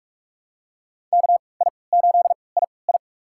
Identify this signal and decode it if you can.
Morse code, heard as a keyed tone, about 44 wpm, spaced as for 25 wpm, 700 Hz. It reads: KI8II
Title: USA Call Signs - 5 Characters - 3x Character Spacing - 45wpm